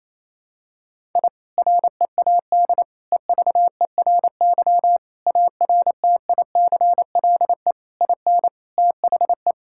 Morse code, heard as a keyed tone, about 28 wpm, 700 Hz. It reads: I READ EVERY ARTICLE IN THE
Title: Sentences from Top 1000 Words - Review Part 5 - 28wpm